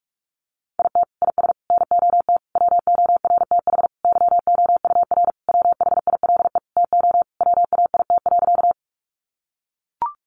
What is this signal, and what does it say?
Morse code, heard as a keyed tone, about 44 wpm, 700 Hz. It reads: IT IS NOT WORTH YOUR WHILE TO WAIT.
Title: Sentences from Top 800 Words 45wpm